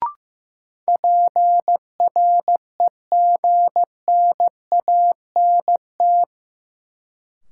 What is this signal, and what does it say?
Morse code, heard as a keyed tone, about 15 wpm, 700 Hz. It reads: PREGNANT
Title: Sentences from Top 1000 Words - Review Part 6 - New Word Drill - 15wpm